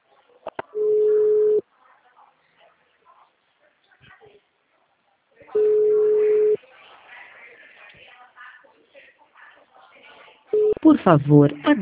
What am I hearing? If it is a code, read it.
Ringback tone (Europe, 425 Hz), with speech.
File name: out- -2004- - - .11347